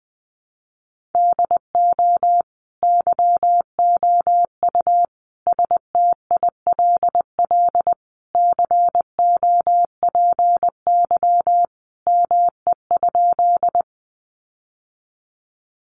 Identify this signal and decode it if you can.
Morse code, heard as a keyed tone, about 20 wpm, 700 Hz. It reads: DO YOU STILL COPY ME?
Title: Sentences from Top 1400 Words 20wpm